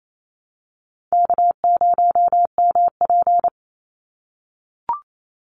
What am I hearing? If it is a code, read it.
Morse code, heard as a keyed tone, about 28 wpm, 700 Hz. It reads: K0MP
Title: NA SOTA Chaser Callsigns 28wpm